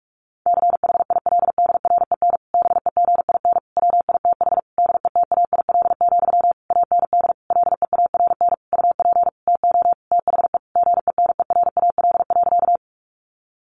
Morse code, heard as a keyed tone, about 44 wpm, 700 Hz. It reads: CHILDREN BEGIN WITH DETAIL, AND LEARN UP TO THE GENERAL.
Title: Sentences from Top 1500 Words 45wpm